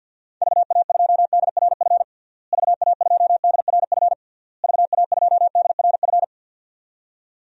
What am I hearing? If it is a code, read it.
Morse code, heard as a keyed tone, about 52 wpm, 700 Hz. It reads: VA2DRF VA2DRF VA2DRF
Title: DXCC-100 Call Signs - 6 Characters - Repeated 3x - 50wpm